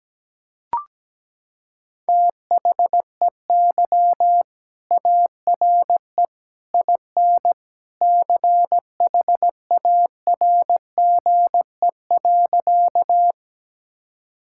Morse code, heard as a keyed tone, about 17 wpm, 700 Hz. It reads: THEY ARE IN CHARGE.
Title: Sentences from Top 1000 Words - Review Part 6 - 17wpm